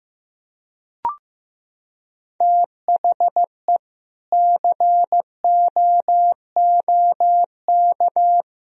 Morse code, heard as a keyed tone, about 15 wpm, 700 Hz. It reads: THE COOK
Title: Sentences from Top 1000 Words - Review Part 4 - 15wpm